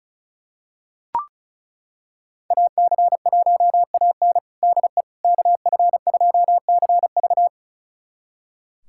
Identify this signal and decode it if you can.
Morse code, heard as a keyed tone, about 36 wpm, 700 Hz. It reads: AC1AN DE KF2CV
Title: USA Callsign DE Callsign - 5 Characters - 35wpm